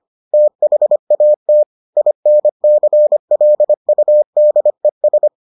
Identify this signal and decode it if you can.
Morse code, heard as a keyed tone, about 25 wpm, 600 Hz. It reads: THAT INCLUDES